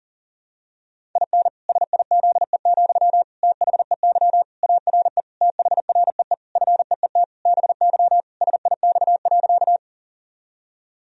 Morse code, heard as a keyed tone, about 40 wpm, 700 Hz. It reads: IN SIZE, THEY ARE THREE FEET BY SIX.
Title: Sentences from Top 1000 Words - Review Part 2 - 40wpm